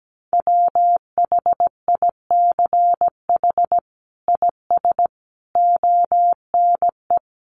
Morse code, heard as a keyed tone, about 17 wpm, 700 Hz. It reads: WHICH IS ONE
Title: Sets of 5 Words Encore 17wpm